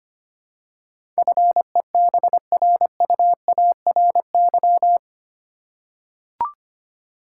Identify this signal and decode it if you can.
Morse code, heard as a keyed tone, about 25 wpm, 690 Hz. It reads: FEBRUARY
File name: 8 Letter Words - First 3 Letters Given - 25wpm